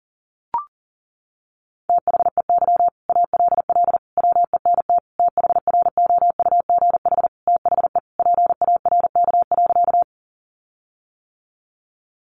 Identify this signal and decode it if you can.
Morse code, heard as a keyed tone, about 40 wpm, 700 Hz. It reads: THEY ALL WENT THROUGH THE PARK.
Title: Sentences from Top 1300 Words 40wpm